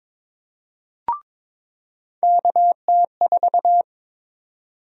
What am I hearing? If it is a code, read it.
Morse code, heard as a keyed tone, about 22 wpm, 700 Hz. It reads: KT4
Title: USA Call Sign Prefixes - 22wpm